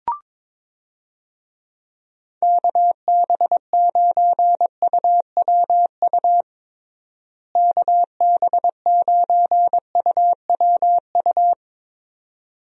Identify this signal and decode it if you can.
Morse code, heard as a keyed tone, about 22 wpm, 700 Hz. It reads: KB9UWU KB9UWU
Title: USA Call Signs - 6 Characters - Repeated 2x - 22wpm